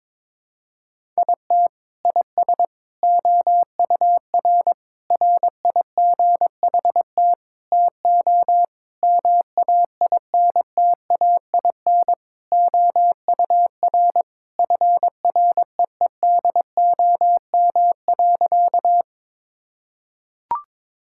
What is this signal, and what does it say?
Morse code, heard as a keyed tone, about 22 wpm, 700 Hz. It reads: IT IS OUR RIGHT TO MAINTAIN OUR FREEDOM.
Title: Sentences from Top 1000 Words - Review Part 5 - 22wpm